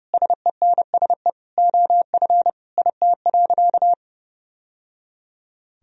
Morse code, heard as a keyed tone, about 32 wpm, 700 Hz. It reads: SENSE OF IT.
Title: Sentences from Top 1000 Words - Review Part 4 - 30wpm